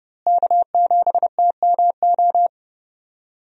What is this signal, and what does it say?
Morse code, heard as a keyed tone, about 30 wpm, 700 Hz. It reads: K7TMO
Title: USA Call Signs - 5 Characters - Repeated 2x - 30wpm